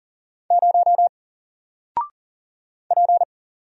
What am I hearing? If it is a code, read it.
Morse code, heard as a keyed tone, about 40 wpm, 700 Hz. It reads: 0 P
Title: Single Letter-Number Alt 40wpm